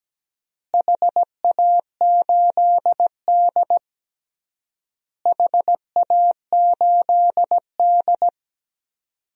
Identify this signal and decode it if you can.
Morse code, heard as a keyed tone, about 17 wpm, 700 Hz. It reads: HA8D HA8D
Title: DXCC-100 Call Signs - 4 Characters - Repeated 3x - 17wpm